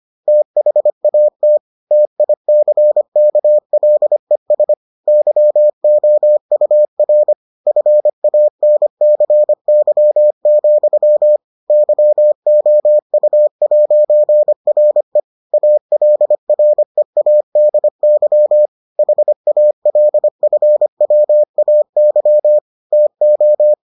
Morse code, heard as a keyed tone, about 25 wpm, 600 Hz. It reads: THAT TICKLES YOUR FANCY, YOU'RE ALREADY HALFWAY TO